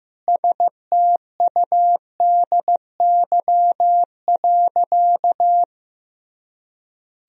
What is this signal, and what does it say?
Morse code, heard as a keyed tone, about 15 wpm, 700 Hz. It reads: STUDY.
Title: Sentences from Top 1000 Words - Review Part 1 - 15wpm